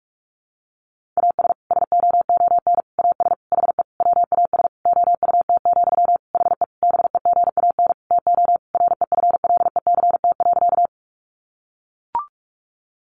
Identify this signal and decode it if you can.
Morse code, heard as a keyed tone, about 44 wpm, 700 Hz. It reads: AS SOON AS HE WAS OUT, HE BEGAN TO REFLECT.
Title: Sentences from Top 1500 Words 45wpm